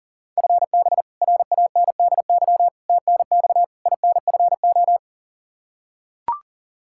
Morse code, heard as a keyed tone, about 40 wpm, 700 Hz. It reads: FB RANDY TNX INFO
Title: Practice QSO - Exchange 3 - More Info - 40wpm